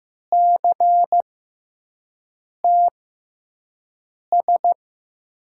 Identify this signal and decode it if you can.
Morse code, heard as a keyed tone, about 16 wpm, spaced as for 5 wpm, 700 Hz. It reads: CTS
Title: USA Call Signs - 6x Character Spacing - 15wpm